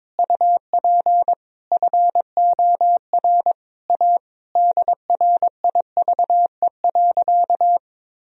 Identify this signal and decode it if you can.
Morse code, heard as a keyed tone, about 22 wpm, 700 Hz. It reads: UP FOR A DRIVE.
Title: Sentences from Top 1000 Words - Review Part 2 - 22wpm